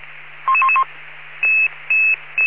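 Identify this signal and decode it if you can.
Five-tone selective call (ZVEI), heard as quick runs of five tones, tones of about 70 ms. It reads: 11111
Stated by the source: ZVEI five-tone selective call received on an SDR.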